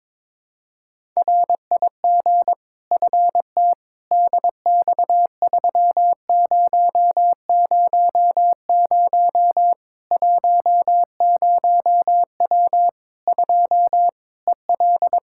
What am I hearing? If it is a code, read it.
Morse code, heard as a keyed tone, about 22 wpm, 700 Hz. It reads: RIG FT DX3000 10W 2 EL
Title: Practice QSO - Exchange 2 - Info - 22wpm